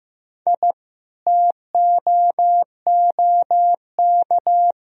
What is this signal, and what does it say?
Morse code, heard as a keyed tone, about 15 wpm, 700 Hz. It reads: I TOOK
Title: Sentences from Top 1000 Words - Review Part 2 - 15wpm